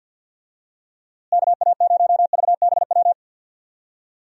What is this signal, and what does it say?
Morse code, heard as a keyed tone, about 51 wpm, 700 Hz. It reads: KA0VBW
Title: USA Call Signs - 6 Characters - Repeated 2x - 50wpm